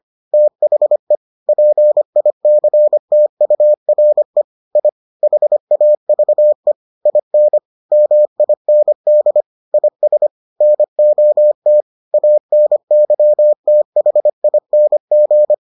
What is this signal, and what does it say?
Morse code, heard as a keyed tone, about 25 wpm, 600 Hz. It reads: THE PICTURE I HAVE IN MIND IS NOT ANYTHING